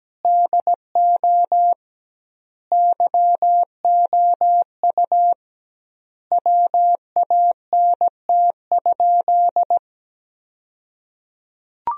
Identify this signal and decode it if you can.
Morse code, heard as a keyed tone, about 17 wpm, 700 Hz. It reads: DO YOU WANT?
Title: 2x Word-Spacing - Sentences from Top 400 Words - 17wpm